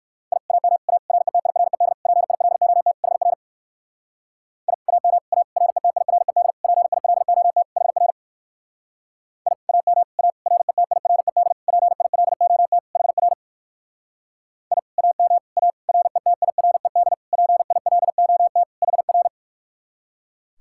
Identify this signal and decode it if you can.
Morse code, heard as a keyed tone, about 59 wpm, 700 Hz. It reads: I AM A RETIRED PILOT HR I AM A RETIRED PILOT HR I AM A RETIRED PILOT HR I AM A RETIRED PILOT HR